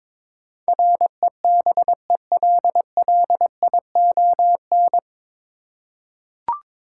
Morse code, heard as a keyed tone, about 22 wpm, 700 Hz. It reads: REBELLION
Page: Morse Code Ninja practice 22wpm